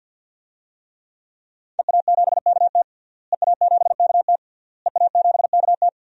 Morse code, heard as a keyed tone, about 51 wpm, 700 Hz. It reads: EA7KT EA7KT EA7KT
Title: DXCC-100 Call Signs - 5 Characters - Repeated 3x - 50wpm